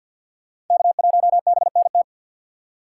Morse code, heard as a keyed tone, about 48 wpm, 700 Hz. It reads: K1BNT
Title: USA Call Signs - 5 Characters - 2x Character Spacing - 50wpm